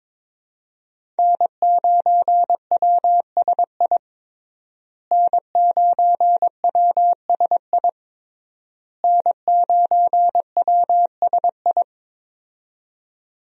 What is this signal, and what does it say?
Morse code, heard as a keyed tone, about 22 wpm, 700 Hz. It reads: N9WSI N9WSI N9WSI
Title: USA Call Signs - 5 Characters - Repeated 3x - 22wpm